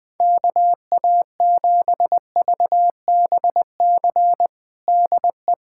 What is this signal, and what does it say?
Morse code, heard as a keyed tone, about 21 wpm, 700 Hz. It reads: KA7VBC DE
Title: USA Callsign DE Callsign - 6 Characters - 20wpm